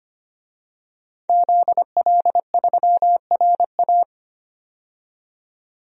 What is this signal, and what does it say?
Morse code, heard as a keyed tone, about 25 wpm, 700 Hz. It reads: ZL3RA
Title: DXCC-100 Call Signs - 5 Characters - 25wpm